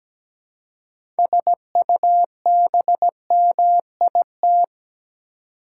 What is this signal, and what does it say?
Morse code, heard as a keyed tone, about 17 wpm, 700 Hz. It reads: SUBMIT
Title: Six Letter Words - 17wpm